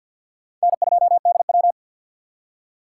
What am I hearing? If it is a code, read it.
Morse code, heard as a keyed tone, about 62 wpm, 700 Hz. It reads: N2DW
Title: Speed Racing - USA 1x2 Call Signs 50wpm